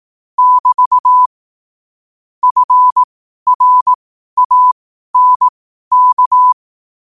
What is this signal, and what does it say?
Morse code, heard as a keyed tone, about 18 wpm, spaced as for 12 wpm, 1000 Hz. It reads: <BT> FRANK
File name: qotdfiles / 12wpm